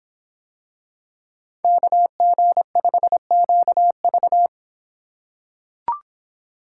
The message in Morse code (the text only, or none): KG5QV